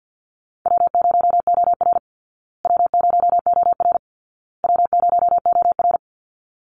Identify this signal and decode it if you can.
Morse code, heard as a keyed tone, about 50 wpm, 700 Hz. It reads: W0OR W0OR W0OR
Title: Easy Call Signs Repeated 3x - 50wpm